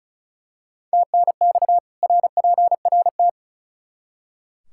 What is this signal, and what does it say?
Morse code, heard as a keyed tone, about 36 wpm, 700 Hz. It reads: TNX RPRT
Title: Practice QSO - Exchange 2 - Info - 35wpm